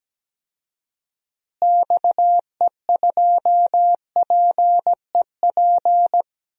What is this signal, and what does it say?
Morse code, heard as a keyed tone, about 17 wpm, 700 Hz. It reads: XE2PEP
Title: DXCC-100 Call Signs - 6 Characters - Repeated 2x - 17wpm